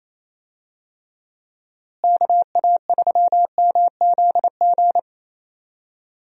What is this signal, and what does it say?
Morse code, heard as a keyed tone, about 28 wpm, 700 Hz. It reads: KA3MZG
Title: USA Call Signs - 6 Characters - Repeated 3x - 28wpm